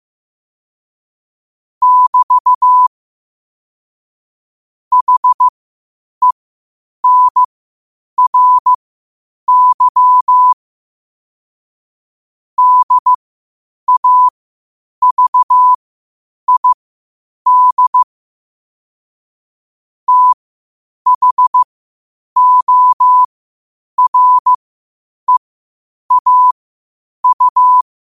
Morse code, heard as a keyed tone, about 15 wpm, spaced as for 8 wpm, 1000 Hz. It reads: <BT> HENRY DAVID THOREAU
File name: qotdfiles / 8wpm